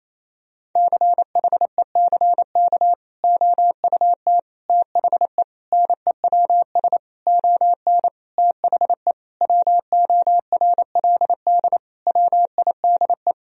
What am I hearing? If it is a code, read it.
Morse code, heard as a keyed tone, about 28 wpm, 700 Hz. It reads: CHECK OUT THE NEWS ON THE WORLD WIDE
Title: Sentences from Top 1000 Words - Review Part 4 - 28wpm